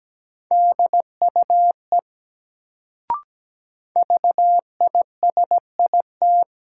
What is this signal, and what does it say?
Morse code, heard as a keyed tone, about 17 wpm, 700 Hz. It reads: DUE VISIT
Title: Sentences from Top 900 Words - New Word Drill - 17wpm